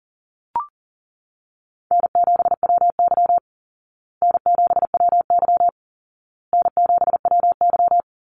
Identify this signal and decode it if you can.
Morse code, heard as a keyed tone, about 40 wpm, 700 Hz. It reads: N7WY N7WY N7WY
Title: Call Signs Repeated 2x - 40wpm